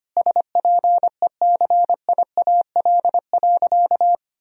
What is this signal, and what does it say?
Morse code, heard as a keyed tone, about 25 wpm, 700 Hz. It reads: SPECIAL.
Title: Sentences from Top 1000 Words - Review Part 3 - 25wpm